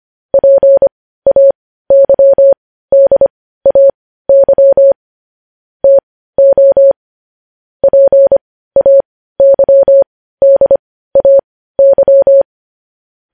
Morse code, heard as a keyed tone, about 25 wpm, spaced as for 15 wpm, 550 Hz. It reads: PAYDAY TO PAYDAY
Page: news-headlines-in-morse-code-at-15-wpm